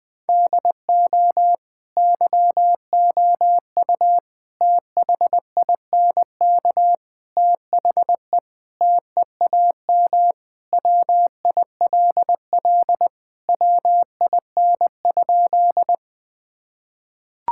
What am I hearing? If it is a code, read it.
Morse code, heard as a keyed tone, about 20 wpm, 700 Hz. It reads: DO YOU THINK THE TEAM WILL WIN?
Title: Sentences with 4-Word Phrases - Spoken Beginning - 20wpm